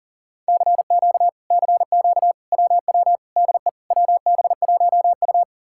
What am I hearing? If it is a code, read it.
Morse code, heard as a keyed tone, about 53 wpm, 700 Hz. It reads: CQ CQ WW DE WB1U